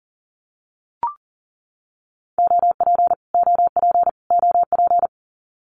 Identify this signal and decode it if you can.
Morse code, heard as a keyed tone, about 40 wpm, 700 Hz. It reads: OP OP OP
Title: Abbreviations Repeated 3x - 40wpm